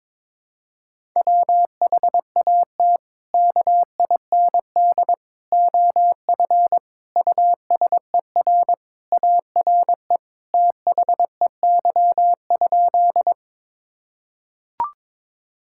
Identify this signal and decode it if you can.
Morse code, heard as a keyed tone, about 22 wpm, 700 Hz. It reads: WHAT KIND OF USER ARE THEY?
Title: Sentences from Top 1000 Words - Review Part 5 - 22wpm